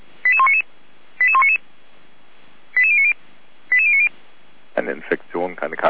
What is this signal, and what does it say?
Five-tone selective call (ZVEI), heard as quick runs of five tones, tones of about 70 ms, with speech. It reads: 88199 88199 88090 88090